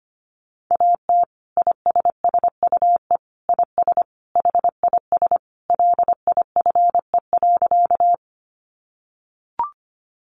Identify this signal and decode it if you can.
Morse code, heard as a keyed tone, about 25 wpm, 700 Hz. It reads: AT ISSUE IS HIS LIFE.
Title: Sentences from Top 900 Words 25wpm